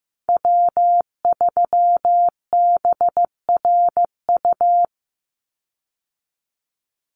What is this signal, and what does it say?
Morse code, heard as a keyed tone, about 15 wpm, 700 Hz. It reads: W3BRU
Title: USA Call Signs 15wpm